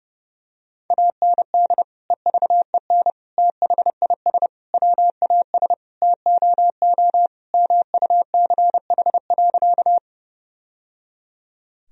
Morse code, heard as a keyed tone, about 32 wpm, 700 Hz. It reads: AND EVEN THIS WAS TOO MUCH.
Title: Sentences from Top 1000 Words - Review Part 1 - 30wpm